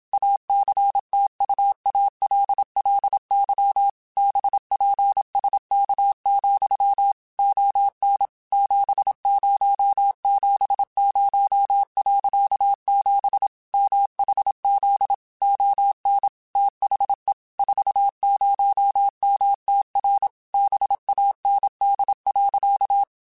Morse code, heard as a keyed tone, about 27 wpm, 800 Hz. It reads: ACTUALLY BPSK, ON 7070.7 MHZ ON THE 40MTR BAND.
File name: QSOs / QSOs-mp3s / WA7QOX 2@25